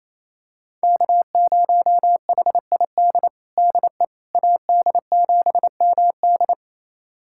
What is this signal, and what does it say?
Morse code, heard as a keyed tone, about 28 wpm, 700 Hz. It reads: K0HID DE AD7MD
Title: USA Callsign DE Callsign - 5 Characters - 28wpm